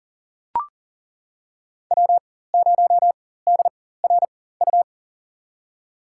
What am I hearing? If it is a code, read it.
Morse code, heard as a keyed tone, about 37 wpm, spaced as for 18 wpm, 700 Hz. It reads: W0DRU
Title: USA Call Signs - 5 Characters - 4x Character Spacing - 40wpm